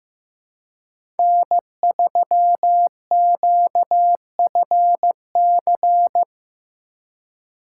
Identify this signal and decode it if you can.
Morse code, heard as a keyed tone, about 18 wpm, 700 Hz. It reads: N3QFC
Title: Speed Racing - USA 1x3 Call Signs 15wpm